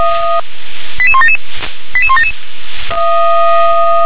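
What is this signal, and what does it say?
Five-tone selective call (ZVEI), heard as quick runs of five tones, tones of about 70 ms. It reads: 88177 88177
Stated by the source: ZVEI five-tone call received by SDR.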